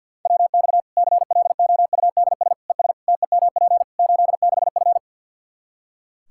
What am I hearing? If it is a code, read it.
Morse code, heard as a keyed tone, about 51 wpm, 700 Hz. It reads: WX CLOUDS ES TEMP 86F
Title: Practice QSO - Exchange 2 - Info - 50wpm